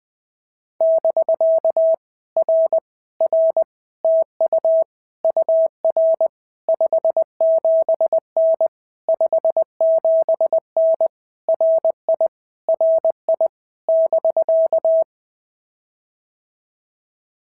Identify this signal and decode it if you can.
Morse code, heard as a keyed tone, about 20 wpm, 690 Hz. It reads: <BK> R R TU UR 57N 57N RI RI <BK>
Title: POTA QSO Practice - 20wpm